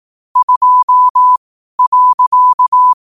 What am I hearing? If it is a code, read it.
Morse code, heard as a keyed tone, about 18 wpm, spaced as for 12 wpm, 1000 Hz. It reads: 2.
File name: qotdfiles / 12wpm